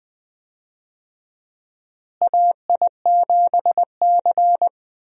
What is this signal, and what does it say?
Morse code, heard as a keyed tone, about 20 wpm, 700 Hz. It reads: AI7C
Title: USA Call Signs - 4 Characters - Repeated 2x - 20wpm